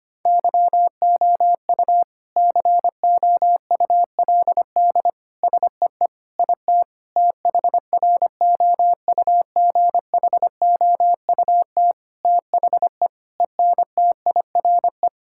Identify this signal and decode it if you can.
Morse code, heard as a keyed tone, about 25 wpm, 700 Hz. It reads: YOU COULD SEE IT THROUGHOUT THE ENTIRE
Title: Sentences from Top 1000 Words - Review Part 5 - 25wpm